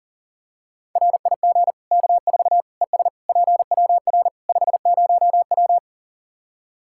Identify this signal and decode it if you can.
Morse code, heard as a keyed tone, about 40 wpm, 700 Hz. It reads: RIG K4 ES PWR 50W
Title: Practice QSO - Exchange 2 - Info - 40wpm